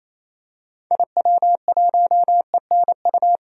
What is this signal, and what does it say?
Morse code, heard as a keyed tone, about 28 wpm, 700 Hz. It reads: IW1ENU
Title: DXCC-100 Call Signs - 6 Characters - 28wpm